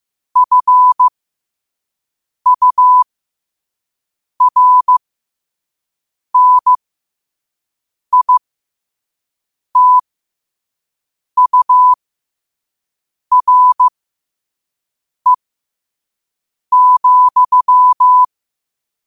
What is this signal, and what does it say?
Morse code, heard as a keyed tone, about 15 wpm, spaced as for 5 wpm, 1000 Hz. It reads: FURNITURE,
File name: qotdfiles / 5wpm